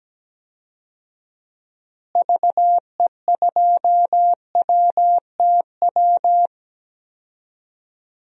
Morse code, heard as a keyed tone, about 17 wpm, 700 Hz. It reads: VE2WTW
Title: DXCC-100 Call Signs - 6 Characters - Repeated 2x - 17wpm